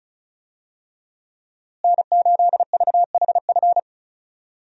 Morse code, heard as a keyed tone, about 36 wpm, 700 Hz. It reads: N8VHF
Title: USA Call Signs - 5 Characters - Repeated 2x - 35wpm